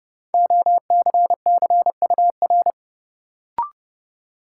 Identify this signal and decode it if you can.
Morse code, heard as a keyed tone, about 30 wpm, 700 Hz. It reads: OCCUR
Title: Sentences from Top 900 Words - New Word Drill - 30wpm